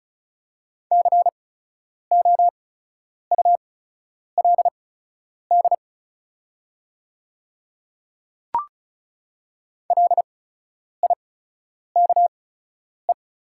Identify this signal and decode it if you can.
Morse code, heard as a keyed tone, about 36 wpm, spaced as for 10 wpm, 690 Hz. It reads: COULD LIKE
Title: 8x Character Spacing - Top 100 Words - 35wpm